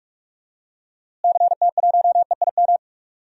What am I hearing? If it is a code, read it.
Morse code, heard as a keyed tone, about 44 wpm, 700 Hz. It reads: CT1EIM
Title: DXCC-100 Call Signs - 6 Characters - Repeated 3x - 45wpm